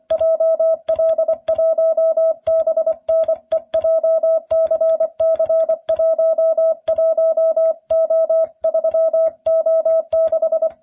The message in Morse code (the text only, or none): JL1BNE JCC11O3O6